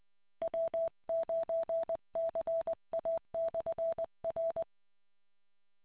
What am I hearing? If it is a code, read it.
Morse code, heard as a keyed tone, about 20 wpm, 660 Hz. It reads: W9CA/R